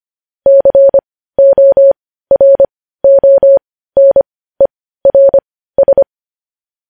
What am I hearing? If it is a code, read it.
Morse code, heard as a keyed tone, about 25 wpm, spaced as for 15 wpm, 550 Hz. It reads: CORONERS
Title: News Headlines at 15